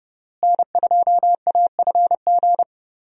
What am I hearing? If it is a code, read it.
Morse code, heard as a keyed tone, about 30 wpm, 700 Hz. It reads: N2AFG
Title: USA Call Signs - 5 Characters - Repeated 3x - 30wpm